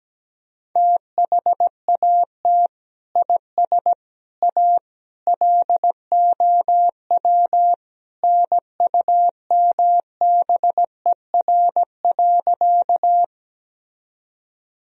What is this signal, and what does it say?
Morse code, heard as a keyed tone, about 17 wpm, 700 Hz. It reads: THAT IS A LOW NUMBER.
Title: Sentences from Top 1000 Words - Review Part 1 - 17wpm